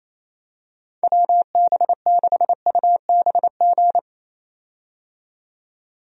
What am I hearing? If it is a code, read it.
Morse code, heard as a keyed tone, about 28 wpm, 700 Hz. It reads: WB6UBG
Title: USA Call Signs - 6 Characters - 28wpm